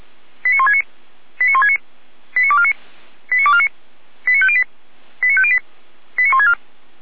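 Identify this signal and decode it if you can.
Five-tone selective call (ZVEI), heard as quick runs of five tones, tones of about 70 ms. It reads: 89169 89169 89249 89249 89508 89508 89164
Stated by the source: ZVEI five-tone call 89250 received by SDR, archived by feuerwehr-orlamuende.org fire brigade.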